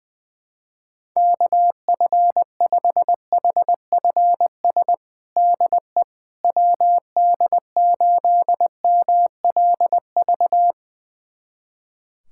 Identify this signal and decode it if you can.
Morse code, heard as a keyed tone, about 20 wpm, 700 Hz. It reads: KF5HFS DE WD8MLV
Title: USA Callsign DE Callsign - 6 Characters - 20wpm